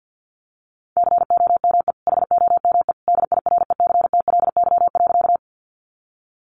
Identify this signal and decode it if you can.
Morse code, heard as a keyed tone, about 50 wpm, 700 Hz. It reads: COME HOME DIRECTLY.
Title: Sentences from Top 800 Words 50wpm